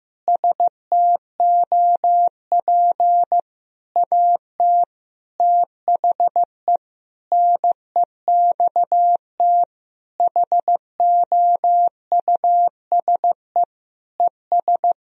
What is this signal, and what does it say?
Morse code, heard as a keyed tone, about 15 wpm, 700 Hz. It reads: STOP AT THE NEXT HOUSE ES
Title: Sentences from Top 400 Words and Abbreviations 15wpm